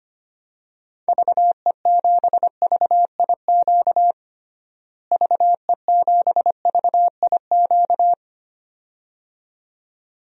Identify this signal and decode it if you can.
Morse code, heard as a keyed tone, about 25 wpm, 700 Hz. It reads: VE7VIQ VE7VIQ
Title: DXCC-100 Call Signs - 6 Characters - Repeated 2x - 25wpm